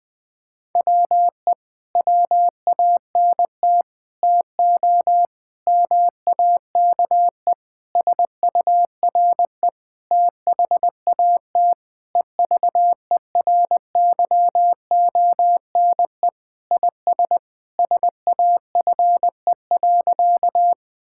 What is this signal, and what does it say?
Morse code, heard as a keyed tone, about 21 wpm, 700 Hz. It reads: WE WANT TO MAKE SURE THAT EVERYONE IS SAFE.
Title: Sentences with 5-Word Phrases - 20wpm